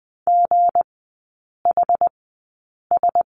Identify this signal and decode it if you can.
Morse code, heard as a keyed tone, about 20 wpm, spaced as for 8 wpm, 690 Hz. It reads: GHS
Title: 159 Lesson 26e - Letter g - Three Characters 20wpm